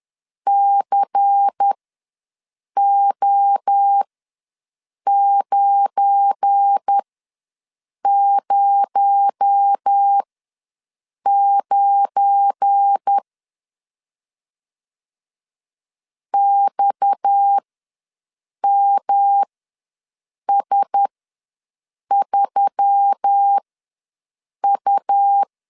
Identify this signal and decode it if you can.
Morse code, heard as a keyed tone, about 10 wpm, spaced as for 6 wpm, 800 Hz. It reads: CO909 XMS3U
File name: LEZIONI / 7 GF9Z / 7 GF9 ws 10wpm 4034